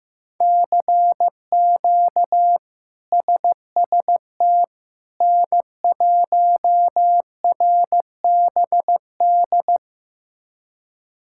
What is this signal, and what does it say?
Morse code, heard as a keyed tone, about 16 wpm, 700 Hz. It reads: CQ SST N1RBD
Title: K1USN SST QSO Practice - 15wpm